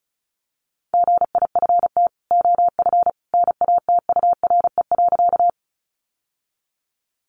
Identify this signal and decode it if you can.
Morse code, heard as a keyed tone, about 34 wpm, 700 Hz. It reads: GIFT OF NATURE.
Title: Sentences from Top 900 Words 35wpm